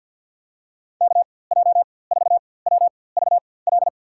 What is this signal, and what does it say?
Morse code, heard as a keyed tone, about 50 wpm, spaced as for 23 wpm, 700 Hz. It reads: KJ4WVL
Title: USA Call Signs - 6 Characters - 4x Character Spacing - 50wpm